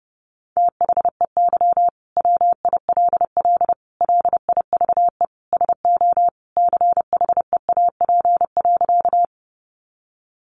Morse code, heard as a keyed tone, about 30 wpm, 700 Hz. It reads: THEY WILL LIVE SO CHEAP.
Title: Sentences from Top 1200 Words 30wpm